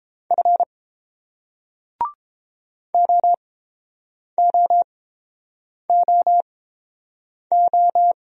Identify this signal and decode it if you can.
Morse code, heard as a keyed tone, about 29 wpm, spaced as for 7 wpm, 700 Hz. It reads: F OOOO